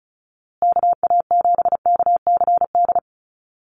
Morse code, heard as a keyed tone, about 34 wpm, 700 Hz. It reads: KA7KCD
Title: USA Call Signs Repeated 3x 35wpm